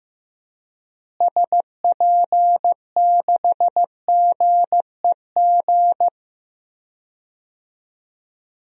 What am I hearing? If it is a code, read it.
Morse code, heard as a keyed tone, about 15 wpm, 700 Hz. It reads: SP6GEG
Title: DXCC-100 Call Signs - 6 Characters - Repeated 3x - 15wpm